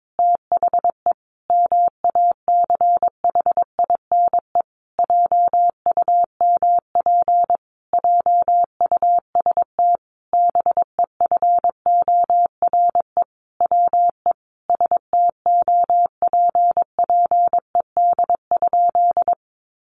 Morse code, heard as a keyed tone, about 22 wpm, 700 Hz. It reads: THE MACHINE JUMP JUST BEFORE WE STOPPED?
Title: A Sound Of Thunder 22wpm